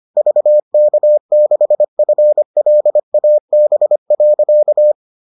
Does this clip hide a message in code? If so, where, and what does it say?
Yes: Morse code from 0.2 to 4.9 s, heard as a keyed tone, about 25 wpm, 600 Hz. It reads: VK6FLAB.